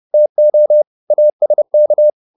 Morse code, heard as a keyed tone, about 30 wpm, 600 Hz. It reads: TO ASK
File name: the old man and the sea / the old man and the sea 1 30wpm